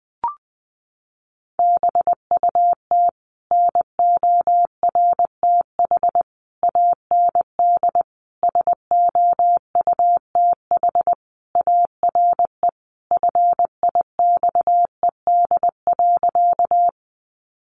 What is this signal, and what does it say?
Morse code, heard as a keyed tone, about 20 wpm, 700 Hz. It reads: BUT NORTH AND SOUTH ARE FIXED.
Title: Sentences from Top 1100 Words 20wpm